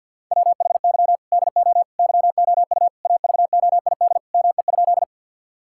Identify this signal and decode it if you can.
Morse code, heard as a keyed tone, about 50 wpm, 700 Hz. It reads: WHY DO YOU AVOID ME?
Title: Sentences from Top 1000 Words - Review Part 6 - 50wpm